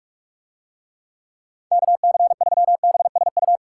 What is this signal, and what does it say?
Morse code, heard as a keyed tone, about 45 wpm, 700 Hz. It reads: KC3BSU